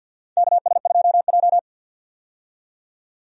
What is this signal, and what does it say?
Morse code, heard as a keyed tone, about 49 wpm, 700 Hz. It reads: KS2J
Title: USA Call Signs - 4 Characters - 50wpm